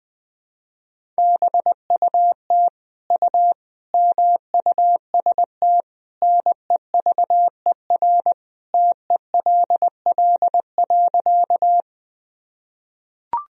Morse code, heard as a keyed tone, about 20 wpm, 700 Hz. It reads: BUT U MUST NEVER TELL.
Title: Sentences from Top 300 Words and Abbreviations 20wpm